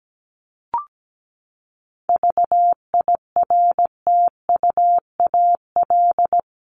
Morse code, heard as a keyed tone, about 17 wpm, 700 Hz. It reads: VIRTUAL
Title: LY Suffix - Top 100 Words with Base Words 17wpm